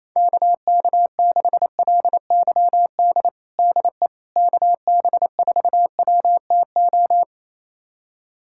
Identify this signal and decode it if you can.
Morse code, heard as a keyed tone, about 28 wpm, 700 Hz. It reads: KK6LYD DE KB4WTO
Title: USA Callsign DE Callsign - 6 Characters - 28wpm